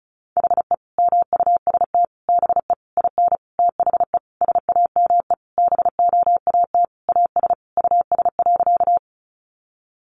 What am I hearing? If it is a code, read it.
Morse code, heard as a keyed tone, about 34 wpm, 700 Hz. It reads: HE MUST BE IN THE SAME BOAT AS US.
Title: Sentences from Top 500 Words 35wpm